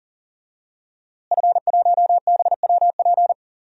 Morse code, heard as a keyed tone, about 40 wpm, 700 Hz. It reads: F1BWP
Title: DXCC-100 Call Signs - 5 Characters - Repeated 2x - 40wpm